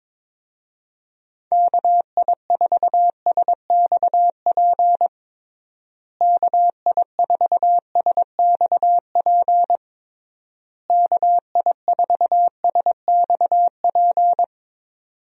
Morse code, heard as a keyed tone, about 22 wpm, 700 Hz. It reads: KI4SXP KI4SXP KI4SXP
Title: USA Call Signs - 6 Characters - Repeated 3x - 22wpm